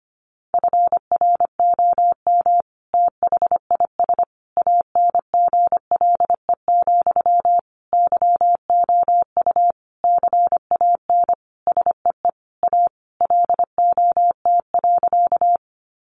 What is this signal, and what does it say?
Morse code, heard as a keyed tone, about 25 wpm, 700 Hz. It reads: FROM THIS ANGLE, YOU CAN SEE A LOT.
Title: Sentences from Top 1500 Words 25wpm